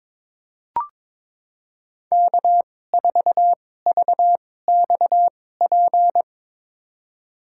Morse code, heard as a keyed tone, about 22 wpm, spaced as for 16 wpm, 700 Hz. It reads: K4VXP
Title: USA Call Signs - 5 Characters - 2x Character Spacing - 22wpm